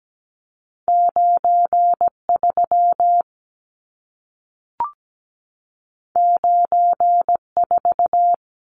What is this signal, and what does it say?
Morse code, heard as a keyed tone, about 17 wpm, 700 Hz. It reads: 93 94
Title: Numbers - 2-Digit 17wpm